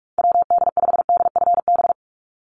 Morse code, heard as a keyed tone, about 44 wpm, 700 Hz. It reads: WD5DFB
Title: USA Call Signs 45wpm